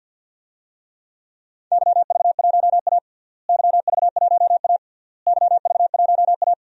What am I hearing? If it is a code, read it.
Morse code, heard as a keyed tone, about 46 wpm, 700 Hz. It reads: YV1A YV1A YV1A
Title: DXCC-100 Call Signs - 4 Characters - Repeated 3x - 50wpm